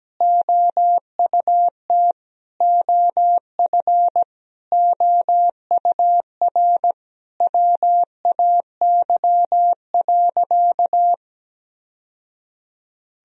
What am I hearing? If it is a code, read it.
Morse code, heard as a keyed tone, about 17 wpm, 700 Hz. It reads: OUT OF OUR WAY.
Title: Sentences from Top 1000 Words - Review Part 1 - 17wpm